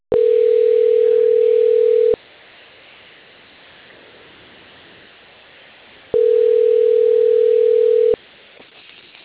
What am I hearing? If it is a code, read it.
Ringback tone (North America), 440 + 480 Hz.